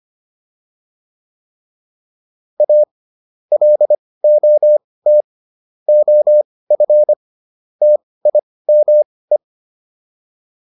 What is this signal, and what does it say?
Morse code, heard as a keyed tone, about 25 wpm, spaced as for 18 wpm, 600 Hz. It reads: A LOT OF TIME